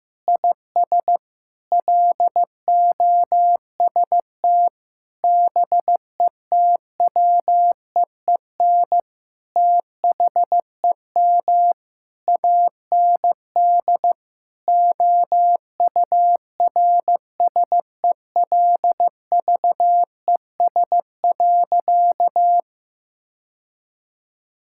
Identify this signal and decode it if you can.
Morse code, heard as a keyed tone, about 15 wpm, 700 Hz. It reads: IS LOST BETWEEN THEM AND OURSELVES.
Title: Sentences from Top 1600 Words 15wpm